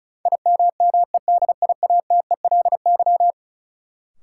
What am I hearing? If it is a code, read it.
Morse code, heard as a keyed tone, about 36 wpm, 700 Hz. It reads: IMMEDIATELY
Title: Sentences from Top 1100 Words - New Word Drill Alt - 35wpm